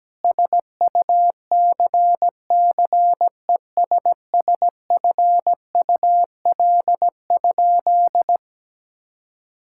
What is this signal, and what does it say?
Morse code, heard as a keyed tone, about 17 wpm, 700 Hz. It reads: SUCCESSFUL?
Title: Sentences from Top 1000 Words - Review Part 6 - 17wpm